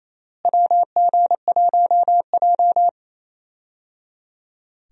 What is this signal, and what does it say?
Morse code, heard as a keyed tone, about 28 wpm, 700 Hz. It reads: WG1J
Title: USA Call Signs - 4 Characters - Repeated 3x - 28wpm